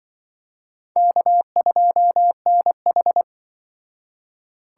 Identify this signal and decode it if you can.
Morse code, heard as a keyed tone, about 26 wpm, 700 Hz. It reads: K2NH